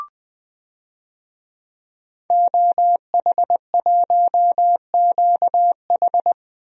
Morse code, heard as a keyed tone, about 20 wpm, 700 Hz. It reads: OH1QH
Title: DXCC-100 Call Signs - 5 Characters - Repeated 3x - 20wpm